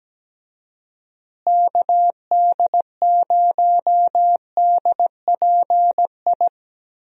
Morse code, heard as a keyed tone, about 17 wpm, 700 Hz. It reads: KD0DPI
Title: USA Call Signs - 6 Characters - Repeated 2x - 17wpm